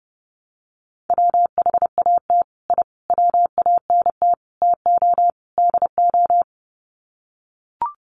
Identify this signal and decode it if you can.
Morse code, heard as a keyed tone, about 30 wpm, 700 Hz. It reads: WHAT I WANT TO DO
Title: Sets of 5 Words Encore 30wpm